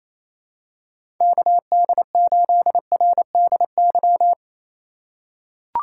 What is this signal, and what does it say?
Morse code, heard as a keyed tone, about 28 wpm, 700 Hz. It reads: KD8RDY
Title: USA Call Signs - 6 Characters - 3x Character Spacing - 28wpm